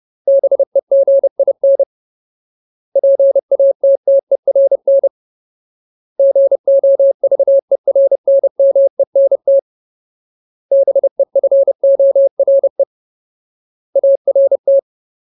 Morse code, heard as a keyed tone, about 30 wpm, 550 Hz. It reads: BEGIN PATTERN GOVERNMENT BEFORE ART
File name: Ultimate Word List / 30WPM Ultimate